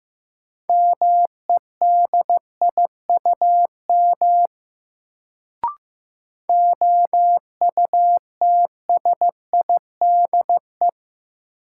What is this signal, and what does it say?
Morse code, heard as a keyed tone, about 15 wpm, 700 Hz. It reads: MEDIUM OUTSIDE
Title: Sentences from Top 900 Words - New Word Drill - 15wpm